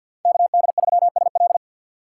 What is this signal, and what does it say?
Morse code, heard as a keyed tone, about 51 wpm, 690 Hz. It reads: KD3SL
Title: USA Call Signs - 5 Characters - 2x Character Spacing - 50wpm